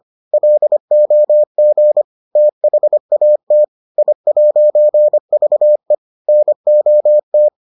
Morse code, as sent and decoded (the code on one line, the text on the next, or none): .-.. --- --. / - .... .- - / .. .----. ...- . / -. --- -
LOG THAT I'VE NOT